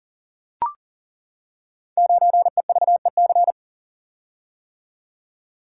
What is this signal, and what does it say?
Morse code, heard as a keyed tone, about 39 wpm, 690 Hz. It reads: 9EVEC
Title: Code Groups - 5 Letter:Numbers - 40wpm